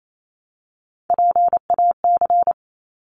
Morse code, heard as a keyed tone, about 28 wpm, 700 Hz. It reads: PAC
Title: ARRL Sections 28wpm